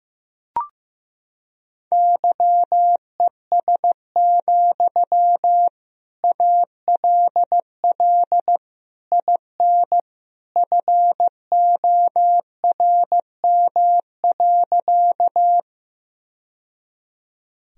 Morse code, heard as a keyed tone, about 16 wpm, 700 Hz. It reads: YES, ALL IN FORM.
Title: Sentences from Top 1000 Words - Review Part 1 - 15wpm